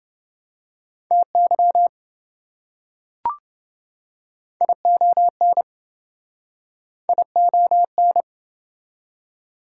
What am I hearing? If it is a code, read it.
Morse code, heard as a keyed tone, about 27 wpm, 700 Hz. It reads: TY ION ION